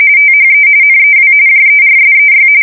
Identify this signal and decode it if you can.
RTTY, about 45.45 baud. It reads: THE NEW JERSEY
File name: fsk170 / USS-New-Jersey